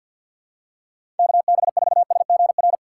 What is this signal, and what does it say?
Morse code, heard as a keyed tone, about 45 wpm, 700 Hz. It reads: KB4SGR
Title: USA Call Signs - 4x Character Spacing - 50wpm